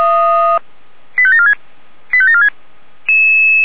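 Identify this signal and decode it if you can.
Five-tone selective call (ZVEI), heard as quick runs of five tones, tones of about 70 ms. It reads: 85746 85746
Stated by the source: ZVEI five-tone selective call 85725, received via SDR.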